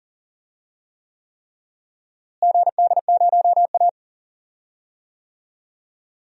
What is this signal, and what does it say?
Morse code, heard as a keyed tone, about 40 wpm, 700 Hz. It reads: GD0A
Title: DXCC-100 Call Signs - 4 Characters - Repeated 3x - 40wpm